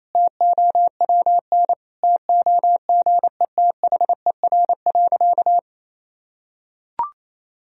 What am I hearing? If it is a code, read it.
Morse code, heard as a keyed tone, about 27 wpm, 700 Hz. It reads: TOWN TOGETHER.
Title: Sentences from Top 1000 Words - Review Part 2 - 28wpm